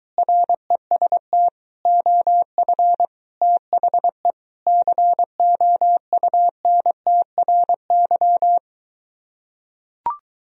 Morse code, heard as a keyed tone, about 23 wpm, 700 Hz. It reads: REST OF THE COUNTRY